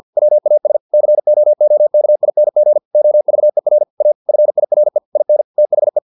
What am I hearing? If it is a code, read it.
Morse code, heard as a keyed tone, about 50 wpm, 600 Hz. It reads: WAS COOKING OVER A FIRE IN THE
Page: Never 49wpm